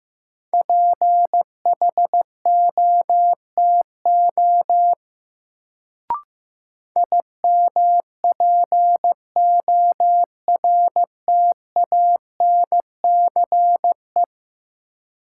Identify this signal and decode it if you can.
Morse code, heard as a keyed tone, about 15 wpm, 700 Hz. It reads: PHOTO IMPORTANCE
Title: Sentences from Top 1200 Words - New Word Drill - 15wpm